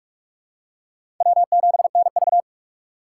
Morse code, heard as a keyed tone, about 46 wpm, 700 Hz. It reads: W7NV
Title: USA Call Signs - 4 Characters - 4x Character Spacing - 45wpm